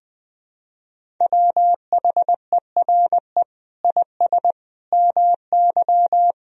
Morse code, heard as a keyed tone, about 20 wpm, 700 Hz. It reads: WHERE IS MY
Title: Sentences from Top 1000 Words - Review Part 2 - 20wpm